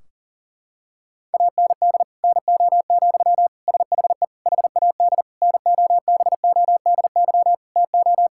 Morse code, heard as a keyed tone, about 40 wpm, 700 Hz. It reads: AND NO, SHE HAD NOBODY TO
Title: Sentences from Top 1600 Words 40wpm